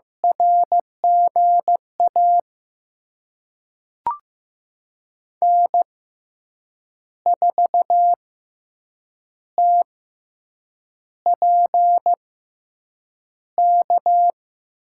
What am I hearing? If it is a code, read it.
Morse code, heard as a keyed tone, about 16 wpm, spaced as for 7 wpm, 700 Hz. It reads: RGA N4TPK
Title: USA Call Signs - 5 Characters - 6x Character Spacing - 15wpm